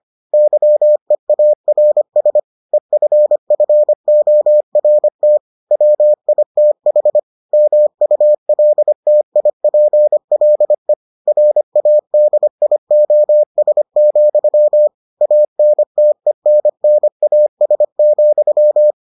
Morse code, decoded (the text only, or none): YEARS EFFORT WITH MULTIPLE RADIOS, ANTENNAS,